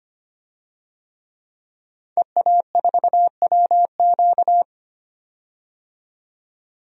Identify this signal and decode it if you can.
Morse code, heard as a keyed tone, about 25 wpm, 700 Hz. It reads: EA4WQ
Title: DXCC-100 Call Signs - 5 Characters - 25wpm